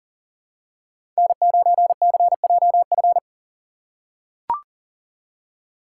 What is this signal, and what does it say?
Morse code, heard as a keyed tone, about 40 wpm, 700 Hz. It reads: N9CJF
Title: USA Call Signs - 5 Characters - 6x Character Spacing - 40wpm